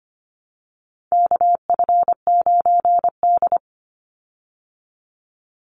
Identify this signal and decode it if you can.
Morse code, heard as a keyed tone, about 24 wpm, 700 Hz. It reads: KF9D
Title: Call Signs Repeated 2x - 25wpm